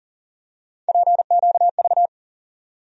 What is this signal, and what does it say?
Morse code, heard as a keyed tone, about 40 wpm, 700 Hz. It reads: PQV
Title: USA Call Sign Postfixes - 3 Letters 40wpm